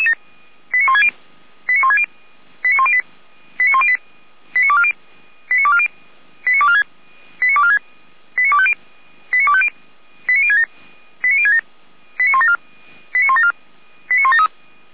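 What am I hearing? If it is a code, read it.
Five-tone selective call (ZVEI), heard as quick runs of five tones, tones of about 70 ms. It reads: 89160 89160 89108 89108 89240 89240 89246 89246 89250 89250 89067 89067 89174 89174 89173
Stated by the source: ZVEI five-tone selective call received on an SDR.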